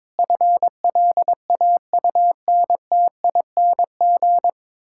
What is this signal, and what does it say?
Morse code, heard as a keyed tone, about 22 wpm, 700 Hz. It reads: FLAUNTING
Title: Pangram Challenge v2 - 22wpm